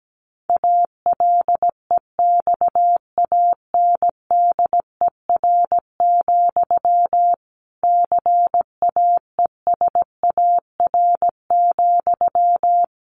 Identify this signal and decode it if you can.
Morse code, heard as a keyed tone, about 17 wpm, 700 Hz. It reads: ALEXANDER, CAESAR,
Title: A Sound Of Thunder 17wpm